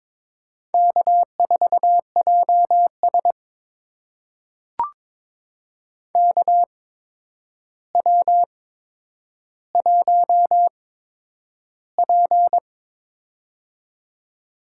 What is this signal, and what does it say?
Morse code, heard as a keyed tone, about 22 wpm, spaced as for 9 wpm, 700 Hz. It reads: K4JS KW1P
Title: USA Call Signs - 4 Characters - 8x Character Spacing - 22wpm